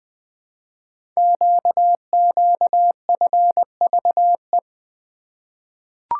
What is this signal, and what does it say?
Morse code, heard as a keyed tone, about 20 wpm, 700 Hz. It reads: QQFVE
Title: Code Groups - 5 Letter:Numbers - 20wpm